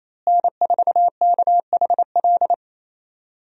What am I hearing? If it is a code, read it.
Morse code, heard as a keyed tone, about 28 wpm, 700 Hz. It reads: N4KHL
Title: USA Call Signs - 5 Characters - Repeated 2x - 28wpm